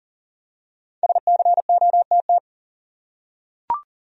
Morse code, heard as a keyed tone, about 40 wpm, 700 Hz. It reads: SCOTT
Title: USA Common Names - 40wpm